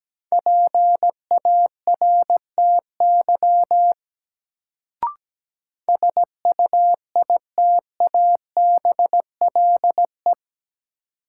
Morse code, heard as a keyed tone, about 17 wpm, 700 Hz. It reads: PARTY SUITABLE
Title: Sentences from Top 1400 Words - New Word Drill - 17wpm